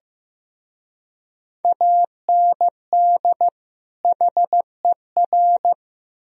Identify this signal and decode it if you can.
Morse code, heard as a keyed tone, about 18 wpm, 700 Hz. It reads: AND HER
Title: Speed-Racing - Sets of 2 Words 15wpm